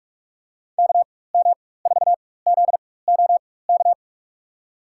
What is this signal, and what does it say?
Morse code, heard as a keyed tone, about 46 wpm, spaced as for 21 wpm, 700 Hz. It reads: KM4ZOK
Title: USA Call Signs - 6 Characters - 4x Character Spacing - 45wpm